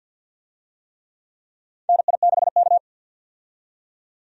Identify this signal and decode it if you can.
Morse code, heard as a keyed tone, about 48 wpm, 700 Hz. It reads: NI6K
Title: USA Call Signs - 4 Characters - 50wpm